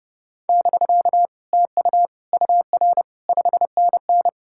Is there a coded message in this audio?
Yes: Morse code from 0.5 to 4.3 s, heard as a keyed tone, about 40 wpm, 700 Hz. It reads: <BK> TU UR 5NN